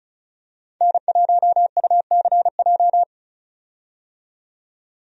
Morse code, heard as a keyed tone, about 34 wpm, 700 Hz. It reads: N1UCJ
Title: USA Call Signs - 5 Characters - 35wpm